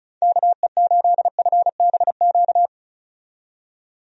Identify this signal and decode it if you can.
Morse code, heard as a keyed tone, about 36 wpm, 700 Hz. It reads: KE8FBQ
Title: USA Call Signs - 6 Characters - 35wpm